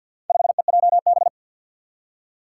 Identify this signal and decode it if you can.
Morse code, heard as a keyed tone, about 51 wpm, 700 Hz. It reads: 5EJB
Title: Four Challenging Letter-Number Combinations 50wpm